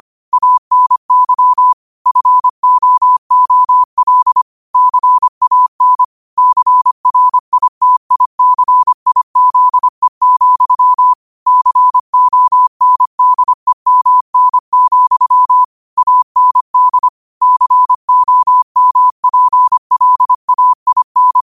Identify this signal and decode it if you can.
Morse code, heard as a keyed tone, about 25 wpm, 1000 Hz. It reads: ANY FOOL CAN CRITICIZE, CONDEMN, AND COMPLAIN
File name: qotdfiles / 25wpm